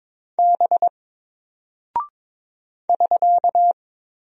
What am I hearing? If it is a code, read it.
Morse code, heard as a keyed tone, about 22 wpm, 700 Hz. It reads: B <SK>
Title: Single Character (Letter:Number:Punctuation:Prosign) Alt 22wpm